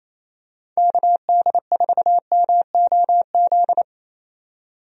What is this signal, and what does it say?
Morse code, heard as a keyed tone, about 27 wpm, 700 Hz. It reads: KD4MOZ
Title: USA Call Signs - 6 Characters - 28wpm